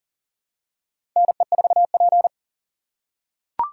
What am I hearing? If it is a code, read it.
Morse code, heard as a keyed tone, about 40 wpm, 700 Hz. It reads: NE4P
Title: USA Call Signs - 4x Character Spacing - 40wpm